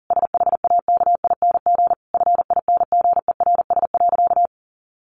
Morse code, heard as a keyed tone, about 40 wpm, 700 Hz. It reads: SHAKING FINGERS.
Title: A Sound Of Thunder 40wpm